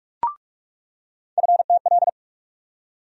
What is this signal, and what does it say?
Morse code, heard as a keyed tone, about 43 wpm, 700 Hz. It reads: FTL
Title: USA Call Sign Postfixes - 3 Letters 45wpm